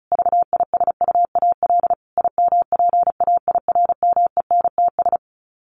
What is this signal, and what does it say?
Morse code, heard as a keyed tone, about 34 wpm, 700 Hz. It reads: VISUAL IMPAIRMENTS
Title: Sets of 5 Words Encore 35wpm